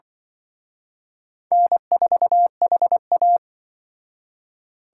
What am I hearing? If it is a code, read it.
Morse code, heard as a keyed tone, about 25 wpm, 700 Hz. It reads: N4HA
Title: Speed Racing - USA Call Signs 20wpm